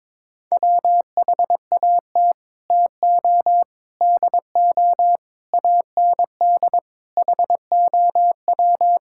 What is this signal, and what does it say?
Morse code, heard as a keyed tone, about 22 wpm, 700 Hz. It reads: WHAT TO DO AND HOW
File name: practice / 22wpm / Sets of 8 Words - 22wpm